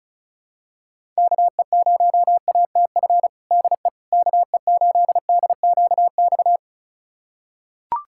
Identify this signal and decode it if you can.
Morse code, heard as a keyed tone, about 34 wpm, 700 Hz. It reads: KE0ATF DE KE8DQX
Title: USA Callsign DE Callsign 35wpm